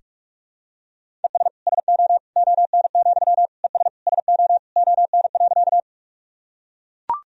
Morse code, heard as a keyed tone, about 44 wpm, 700 Hz. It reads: ES SO ON, ES SO ON.
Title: Sentences from Top 100 Words and Abbreviations 45wpm